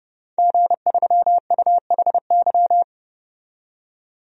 Morse code, heard as a keyed tone, about 32 wpm, 700 Hz. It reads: G3UHY
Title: DXCC-100 Call Signs - 5 Characters - 30wpm